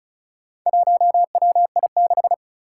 Morse code, heard as a keyed tone, about 36 wpm, 700 Hz. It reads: 1WI6
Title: Four Challenging Letter-Number Combinations 35wpm